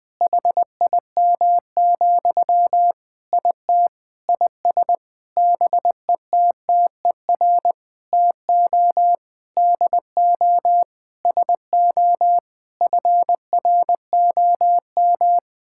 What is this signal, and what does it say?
Morse code, heard as a keyed tone, about 21 wpm, 700 Hz. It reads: HIM, IT IS BETTER TO DO SO FROM
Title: Sentences from Top 1600 Words 20wpm